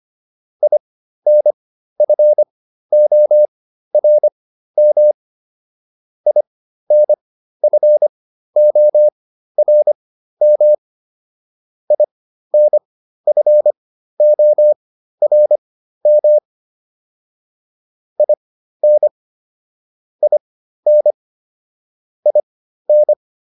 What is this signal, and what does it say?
Morse code, heard as a keyed tone, about 25 wpm, spaced as for 13 wpm, 600 Hz. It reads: INFORM INFORM INFORM IN IN IN